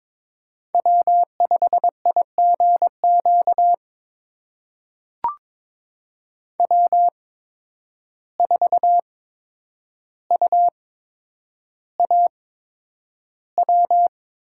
Morse code, heard as a keyed tone, about 22 wpm, spaced as for 9 wpm, 700 Hz. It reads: W5IGQ W4UAW
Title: USA Call Signs - 8x Character Spacing - 22wpm